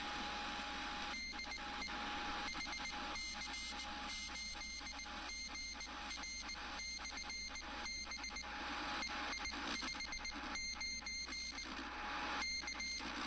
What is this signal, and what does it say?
Morse code, heard as a keyed tone, about 18 wpm, 2100 Hz, with noise. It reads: DE HC8GR/B EI59 K